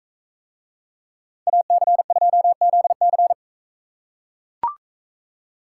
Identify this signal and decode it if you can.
Morse code, heard as a keyed tone, about 43 wpm, 700 Hz. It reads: AC2ZC